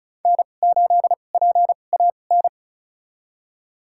Morse code, heard as a keyed tone, about 34 wpm, spaced as for 25 wpm, 700 Hz. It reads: N8PAN
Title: USA Call Signs - 5 Characters - 2x Character Spacing - 35wpm